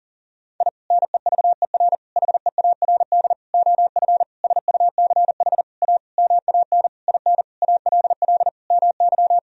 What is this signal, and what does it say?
Morse code, heard as a keyed tone, about 40 wpm, 700 Hz. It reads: I NEVER HEARD OF SUCH A MAN IN ALL MY
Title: Sentences from Top 1000 Words - Review Part 3 - 40wpm